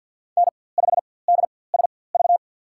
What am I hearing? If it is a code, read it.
Morse code, heard as a keyed tone, about 51 wpm, spaced as for 23 wpm, 700 Hz. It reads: N5DSV
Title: USA Call Signs - 5 Characters - 4x Character Spacing - 50wpm